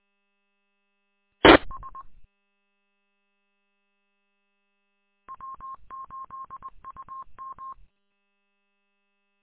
Morse code, heard as a keyed tone, about 20 wpm, 1070 Hz. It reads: TS W8UM
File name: w8um- -combined